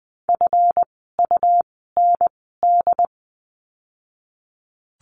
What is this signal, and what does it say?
Morse code, heard as a keyed tone, about 20 wpm, spaced as for 15 wpm, 700 Hz. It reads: FUND
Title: 139 Lesson 23f - Letter f - Words 20wpm